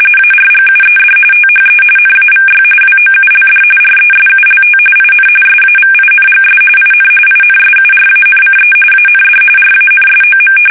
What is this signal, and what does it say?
RTTY, about 45.45 baud. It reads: ARGUED THE NAVY WOULD RESULT IN SAVINGS IN INSURANCE AND RANSOM
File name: fsk850 / usnavy1